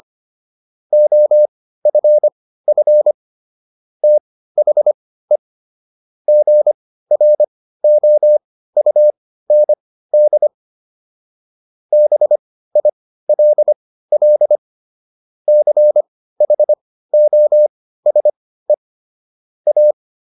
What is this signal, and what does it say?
Morse code, heard as a keyed tone, about 25 wpm, spaced as for 15 wpm, 600 Hz. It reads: OFF THE GROUND BILL CHOSE A